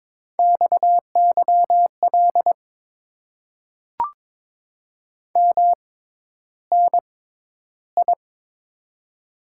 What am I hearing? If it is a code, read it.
Morse code, heard as a keyed tone, about 21 wpm, spaced as for 11 wpm, 700 Hz. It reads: XYL MNI
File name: Top 90 QSO Elements - 6x Character Spacing - 22wpm